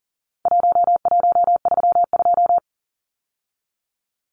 Morse code, heard as a keyed tone, about 40 wpm, 700 Hz. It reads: 1132
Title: Numbers - Contest Style - 4-Digit 40wpm